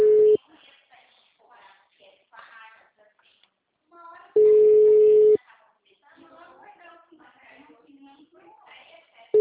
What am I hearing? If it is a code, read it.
Ringback tone (Europe, 425 Hz), with speech.
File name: out- -2003- - - .104